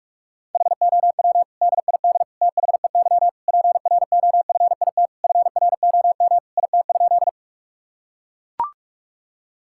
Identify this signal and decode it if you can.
Morse code, heard as a keyed tone, about 41 wpm, 700 Hz. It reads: HOW DID THEY PROFIT FROM IT?
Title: Sentences from Top 1000 Words - Review Part 5 - 45wpm